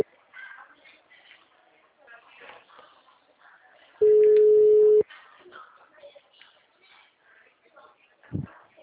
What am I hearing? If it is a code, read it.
Ringback tone (Europe, 425 Hz), with speech.